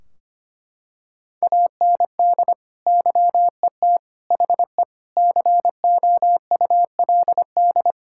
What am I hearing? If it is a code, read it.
Morse code, heard as a keyed tone, about 25 wpm, 700 Hz. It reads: AND YET HE COULD
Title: Sentences from Top 1600 Words 25wpm